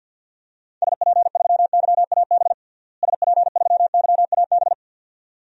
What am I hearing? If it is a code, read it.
Morse code, heard as a keyed tone, about 50 wpm, 700 Hz. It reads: SP3YAB SP3YAB
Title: DXCC-100 Call Signs - 6 Characters - Repeated 2x - 50wpm